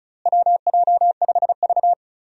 Morse code, heard as a keyed tone, about 36 wpm, 700 Hz. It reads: WJ5V
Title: USA Call Signs - 4 Characters - Repeated 3x - 35wpm